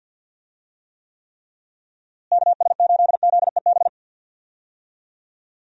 Morse code, heard as a keyed tone, about 52 wpm, 700 Hz. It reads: KS8ZEB
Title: USA Call Signs - 6 Characters - Repeated 3x - 50wpm